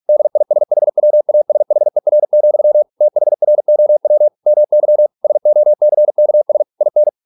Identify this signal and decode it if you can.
Morse code, heard as a keyed tone, about 46 wpm, 600 Hz. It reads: DISHWASHER, THROW MY SOCKS IN